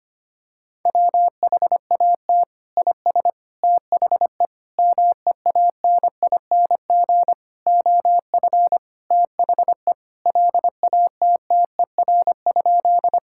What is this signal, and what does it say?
Morse code, heard as a keyed tone, about 25 wpm, 700 Hz. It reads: WHAT IS THE MEANING OF THE LATTER?
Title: Sentences from Top 1600 Words 25wpm